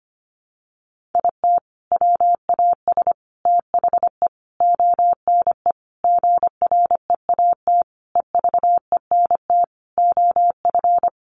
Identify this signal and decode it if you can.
Morse code, heard as a keyed tone, about 25 wpm, 700 Hz. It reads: IT WAS THE ONE GREAT EVENT OF
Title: Sentences from Top 1400 Words 25wpm